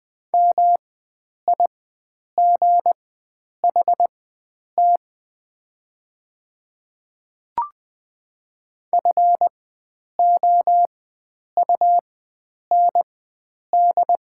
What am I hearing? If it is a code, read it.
Morse code, heard as a keyed tone, about 20 wpm, spaced as for 9 wpm, 690 Hz. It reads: MIGHT FOUND
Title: 4x Character Spacing - Top 300 Words - 20wpm